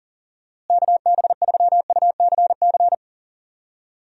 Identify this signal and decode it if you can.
Morse code, heard as a keyed tone, about 40 wpm, 700 Hz. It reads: KB3UCC
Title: USA Call Signs - 6 Characters - Repeated 3x - 40wpm